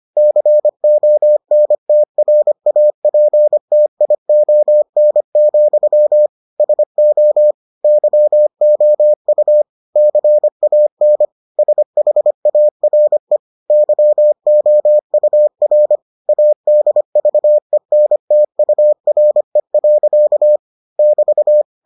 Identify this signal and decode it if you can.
Morse code, heard as a keyed tone, about 25 wpm, 600 Hz. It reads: CONTRAPTION, SO YOU CAN SHARE YOUR ADVENTURE. <BT>